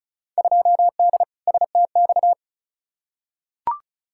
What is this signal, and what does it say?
Morse code, heard as a keyed tone, about 34 wpm, 700 Hz. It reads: 2D STX
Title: Field Day Exchanges - 35wpm